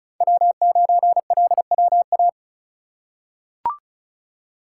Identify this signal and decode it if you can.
Morse code, heard as a keyed tone, about 36 wpm, 700 Hz. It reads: W9LWA
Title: USA Call Signs - 5 Characters - 2x Character Spacing - 35wpm